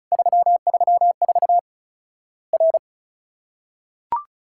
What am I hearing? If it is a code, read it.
Morse code, heard as a keyed tone, about 36 wpm, 700 Hz. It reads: 334 R
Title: Worked All Europe DX Contest - QTC Exchange - 35wpm